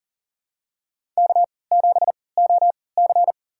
Morse code, heard as a keyed tone, about 37 wpm, spaced as for 22 wpm, 700 Hz. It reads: K7OC
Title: USA Call Signs - 4 Characters - 3x Character Spacing - 40wpm